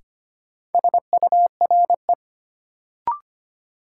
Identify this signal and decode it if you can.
Morse code, heard as a keyed tone, about 25 wpm, 700 Hz. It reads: SURE
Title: Sentences from Top 300 Words - New Word Drill - 25wpm